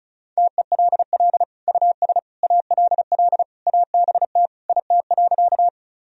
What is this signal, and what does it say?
Morse code, heard as a keyed tone, about 36 wpm, 700 Hz. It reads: TELL US ALL ABT IT.
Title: Sentences from Top 200 Words and Abbreviations 35wpm